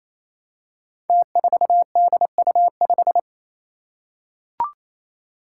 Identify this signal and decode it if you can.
Morse code, heard as a keyed tone, about 29 wpm, 700 Hz. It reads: T4DU5
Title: Five Challenging Letter-Number Combinations 28wpm